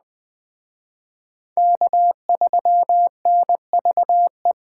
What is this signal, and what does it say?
Morse code, heard as a keyed tone, about 24 wpm, 700 Hz. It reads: K3NVE
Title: Speed Racing - USA 1x3 Call Signs 20wpm